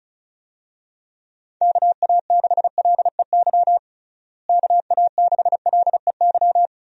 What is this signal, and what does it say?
Morse code, heard as a keyed tone, about 36 wpm, 700 Hz. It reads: KA6LEY KA6LEY
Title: USA Call Signs - 6 Characters - Repeated 2x - 35wpm